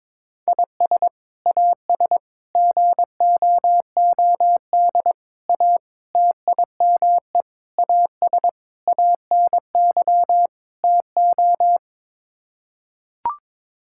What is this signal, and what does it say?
Morse code, heard as a keyed tone, about 22 wpm, 700 Hz. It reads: IS AS GOOD A TIME AS ANY TO
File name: practice / 22wpm / Sets of 8 Words - 22wpm